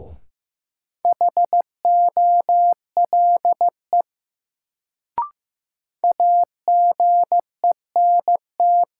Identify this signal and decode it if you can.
Morse code, heard as a keyed tone, about 16 wpm, 700 Hz. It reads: HOLE AGENT
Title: Sentences from Top 1600 Words - New Word Drill Alt - 15wpm